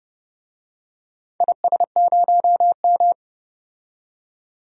Morse code, heard as a keyed tone, about 31 wpm, 700 Hz. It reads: IS0M
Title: DXCC-100 Call Signs - 4 Characters - 30wpm